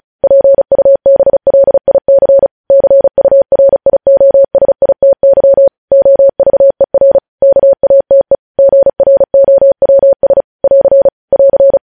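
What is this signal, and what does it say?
Morse code, heard as a keyed tone, about 36 wpm, 550 Hz. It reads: PUBLIC CURIOSITY OVER KATE GROWS <AR> <AR>
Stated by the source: News headlines sent in Morse at speed 35.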